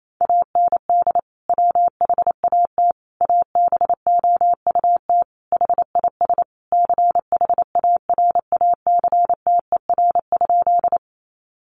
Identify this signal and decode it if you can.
Morse code, heard as a keyed tone, about 28 wpm, 700 Hz. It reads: AND WHAT ABOUT HIS CHARACTER?
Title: Sentences from Top 1200 Words 28wpm